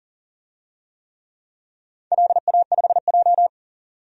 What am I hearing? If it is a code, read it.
Morse code, heard as a keyed tone, about 40 wpm, 700 Hz. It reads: LA5J
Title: DXCC-100 Call Signs - 4 Characters - 40wpm